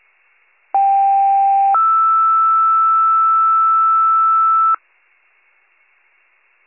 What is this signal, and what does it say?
Two-tone page, heard as a one-second tone, then a three-second tone, about 790 then 1320 Hz. It is heard from 0.7 to 4.7 s.